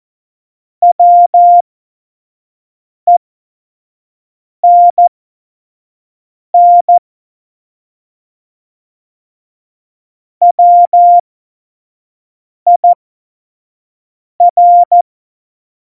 Morse code, heard as a keyed tone, about 14 wpm, spaced as for 5 wpm, 700 Hz. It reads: WENN WIR